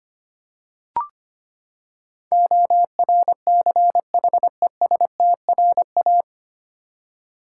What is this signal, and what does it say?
Morse code, heard as a keyed tone, about 25 wpm, 700 Hz. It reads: ORCHESTRA
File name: Nine Letter Words - 25wpm